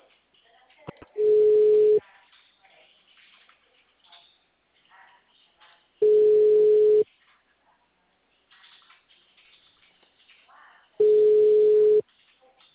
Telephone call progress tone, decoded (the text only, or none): ringback tone